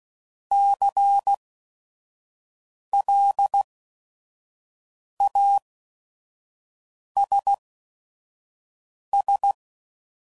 Morse code, heard as a keyed tone, about 16 wpm, spaced as for 5 wpm, 800 Hz. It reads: CLASS
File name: CW Code Groups @5WPM / QSOs / kc7haa